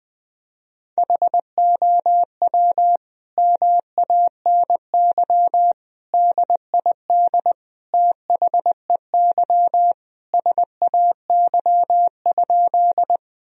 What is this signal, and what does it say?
Morse code, heard as a keyed tone, about 20 wpm, 700 Hz. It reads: HOW MANY DID THEY SAY?
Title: Sentences from Top 1000 Words - Review Part 1 - 20wpm